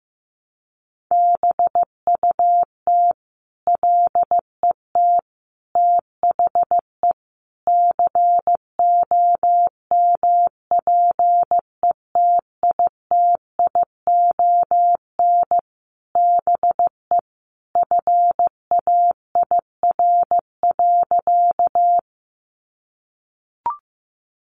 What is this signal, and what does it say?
Morse code, heard as a keyed tone, about 14 wpm, 700 Hz. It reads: BUT LET THE COMPETITION BE FAIR.
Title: Sentences from Top 1300 Words 15wpm